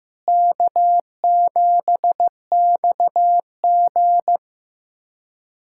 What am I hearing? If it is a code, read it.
Morse code, heard as a keyed tone, about 15 wpm, 700 Hz. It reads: K7XG
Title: USA Callsign DE Callsign - 4 Characters - 15wpm